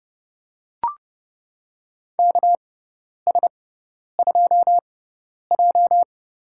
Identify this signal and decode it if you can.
Morse code, heard as a keyed tone, about 29 wpm, spaced as for 10 wpm, 700 Hz. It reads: KS2J
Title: USA Call Signs - 4 Characters - 6x Character Spacing - 30wpm